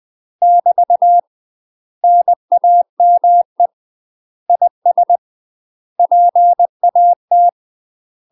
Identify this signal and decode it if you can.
Morse code, heard as a keyed tone, about 20 wpm, 700 Hz. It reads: <BT> NAME IS PAT
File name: UK MASTER QSOs TRAINING / 20WPM / UK MASTER QSOs 0049